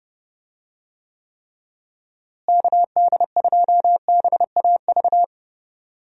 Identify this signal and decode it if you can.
Morse code, heard as a keyed tone, about 32 wpm, 700 Hz. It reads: KD2BAV
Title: USA Call Signs - 6 Characters - Repeated 3x - 30wpm